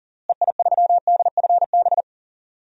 Morse code, heard as a keyed tone, about 37 wpm, 700 Hz. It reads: EI3DFB
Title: DXCC-100 Call Signs - 6 Characters - Repeated 3x - 40wpm